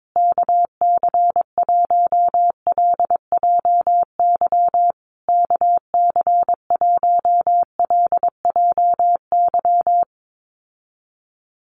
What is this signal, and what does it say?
Morse code, heard as a keyed tone, about 22 wpm, 700 Hz. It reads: KC1LJY KC1LJY
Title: USA Call Signs Repeated 3x 22wpm